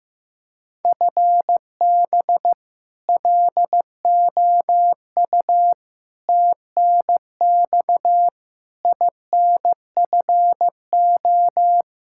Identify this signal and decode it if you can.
Morse code, heard as a keyed tone, about 16 wpm, 700 Hz. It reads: FB LOU TNX INFO
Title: K1USN SST QSO Practice 15wpm